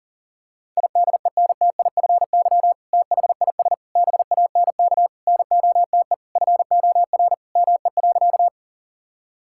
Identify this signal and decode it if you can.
Morse code, heard as a keyed tone, about 40 wpm, 700 Hz. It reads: IDENTIFY THIS BANK NOTE FOR ME.
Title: Sentences from Top 1000 Words - Review Part 6 - 40wpm